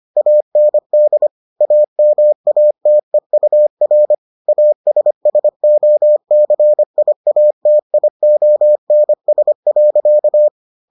Morse code, heard as a keyed tone, about 25 wpm, 600 Hz. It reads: AND AMATEUR ASSOCIATIONS.